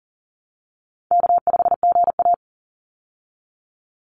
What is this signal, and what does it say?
Morse code, heard as a keyed tone, about 40 wpm, 700 Hz. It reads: K5GA
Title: Easy Call Signs Repeated 3x - 40wpm